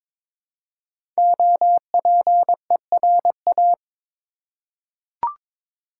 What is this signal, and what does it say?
Morse code, heard as a keyed tone, about 23 wpm, 700 Hz. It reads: OPERA
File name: 5 Letter Words - First 3 Letters Given - 22wpm